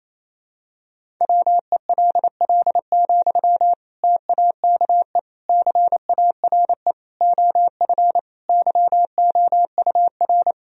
Morse code, heard as a keyed tone, about 28 wpm, 700 Hz. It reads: WELL, TAKE CARE OF YOUR
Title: Sentences from Top 1000 Words - Review Part 1 - 28wpm